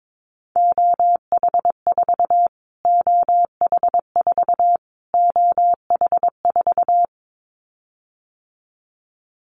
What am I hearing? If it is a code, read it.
Morse code, heard as a keyed tone, about 22 wpm, 700 Hz. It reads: OH4 OH4 OH4
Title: Callsign Prefixes for Top 25 DXCC - Repeated 3x - 22wpm